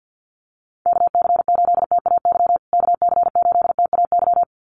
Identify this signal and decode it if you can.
Morse code, heard as a keyed tone, about 50 wpm, 700 Hz. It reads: KC8TAY KC8TAY
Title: USA Call Signs Repeated 2x 50wpm